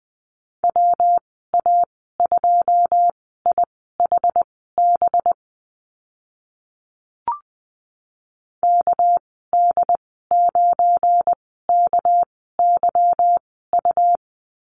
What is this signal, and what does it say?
Morse code, heard as a keyed tone, about 20 wpm, spaced as for 14 wpm, 700 Hz. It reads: WA2IHB KD9KYU
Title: 238 Lesson 38g - Letter q - Call Signs 20wpm